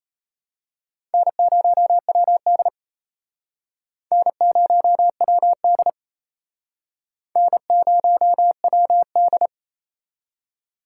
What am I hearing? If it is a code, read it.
Morse code, heard as a keyed tone, about 32 wpm, 700 Hz. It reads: N0WD N0WD N0WD
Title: Speed Racing - USA Call Signs 28wpm